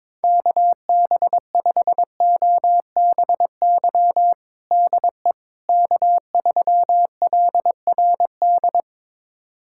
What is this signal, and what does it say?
Morse code, heard as a keyed tone, about 21 wpm, 700 Hz. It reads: KB5OBY DE K3LRD
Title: USA Callsign DE Callsign 22wpm